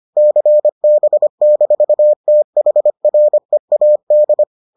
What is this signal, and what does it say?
Morse code, heard as a keyed tone, about 25 wpm, 600 Hz. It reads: CB-THREAD